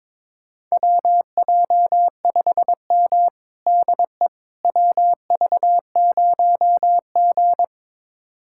Morse code, heard as a keyed tone, about 22 wpm, 700 Hz. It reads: WJ5M DE WV0G
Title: USA Callsign DE Callsign - 4 Characters - 22wpm